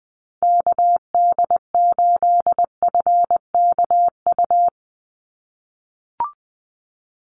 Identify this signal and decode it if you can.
Morse code, heard as a keyed tone, about 20 wpm, 700 Hz. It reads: KD8FKU
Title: USA Call Signs 20wpm